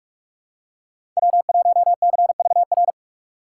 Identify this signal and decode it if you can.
Morse code, heard as a keyed tone, about 46 wpm, 700 Hz. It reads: W1CVR
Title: USA Call Signs - 5 Characters - 4x Character Spacing - 45wpm